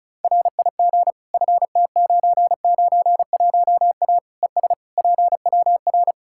Morse code, heard as a keyed tone, about 36 wpm, 700 Hz. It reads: RIG FT991A ES PWR
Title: K1USN SST QSO Practice 35wpm